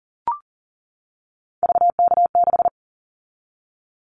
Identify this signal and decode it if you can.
Morse code, heard as a keyed tone, about 40 wpm, 700 Hz. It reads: VK6
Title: Callsign Prefixes for Top 50 DXCC - 40wpm